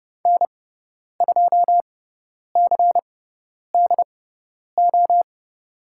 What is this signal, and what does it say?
Morse code, heard as a keyed tone, about 30 wpm, spaced as for 10 wpm, 700 Hz. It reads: N2CDO
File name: Lesson 16g - Letter c - Call Signs 30wpm